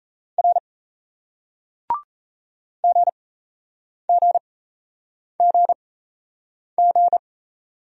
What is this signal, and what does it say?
Morse code, heard as a keyed tone, about 38 wpm, spaced as for 8 wpm, 700 Hz. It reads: R GGGG